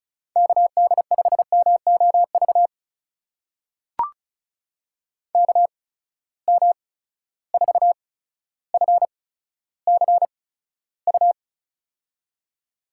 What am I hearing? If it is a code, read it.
Morse code, heard as a keyed tone, about 36 wpm, spaced as for 15 wpm, 700 Hz. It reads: KD5MOV KM4FCU
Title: USA Call Signs - 6 Characters - 8x Character Spacing - 35wpm